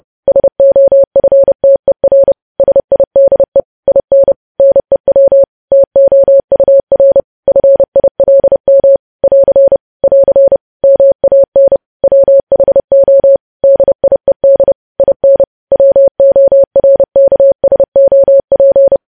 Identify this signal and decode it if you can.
Morse code, heard as a keyed tone, about 30 wpm, 550 Hz. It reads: SOFTER SIDE IN NEW TOUR FILM <AR> <AR> MAN WHO DIED IN WORKSOP
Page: news headlines in morse code at 30 wpm